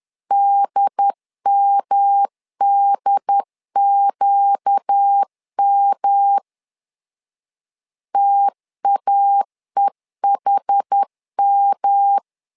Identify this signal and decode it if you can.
Morse code, heard as a keyed tone, about 10 wpm, 800 Hz. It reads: DMDQM TAEHM
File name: LEZIONI / 4 Q 10wpm 2210